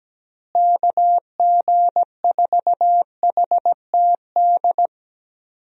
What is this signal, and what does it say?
Morse code, heard as a keyed tone, about 17 wpm, 700 Hz. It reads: KG4HTD
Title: USA Call Signs - 6 Characters - Repeated 2x - 17wpm